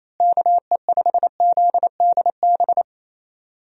Morse code, heard as a keyed tone, about 28 wpm, 700 Hz. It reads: KE5ZDB
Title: USA Call Signs - 6 Characters - Repeated 2x - 28wpm